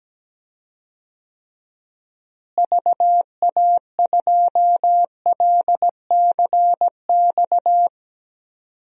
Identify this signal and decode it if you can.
Morse code, heard as a keyed tone, about 17 wpm, 700 Hz. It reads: VA2LCX
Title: DXCC-100 Call Signs - 6 Characters - Repeated 2x - 17wpm